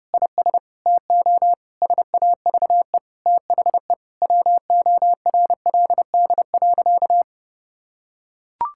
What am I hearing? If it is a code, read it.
Morse code, heard as a keyed tone, about 30 wpm, 700 Hz. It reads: IS TO SAVE THE WORLD.
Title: Sentences from Top 1600 Words 30wpm